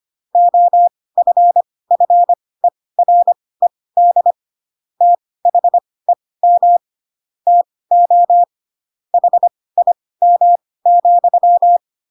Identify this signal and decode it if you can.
Morse code, heard as a keyed tone, about 25 wpm, spaced as for 18 wpm, 700 Hz. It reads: OFFERED THEM TO HIM,
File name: Pinocchio / Pinocchio ch7 12 25 ik1hge